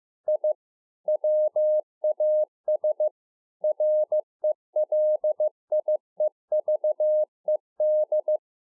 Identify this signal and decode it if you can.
Morse code, heard as a keyed tone, about 15 wpm, 610 Hz. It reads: I WAS RELIEVED